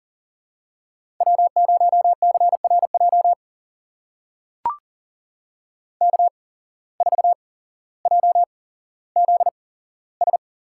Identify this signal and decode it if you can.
Morse code, heard as a keyed tone, about 40 wpm, spaced as for 17 wpm, 700 Hz. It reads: W0CRJ K4JZS
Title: USA Call Signs - 5 Characters - 8x Character Spacing - 40wpm